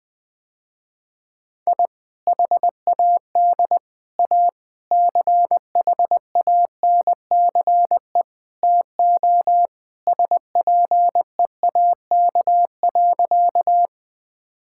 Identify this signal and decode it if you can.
Morse code, heard as a keyed tone, about 21 wpm, 700 Hz. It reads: I HAD A CHANCE TO SPEAK.
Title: Sentences with 4-Word Phrases - Spoken Beginning - 20wpm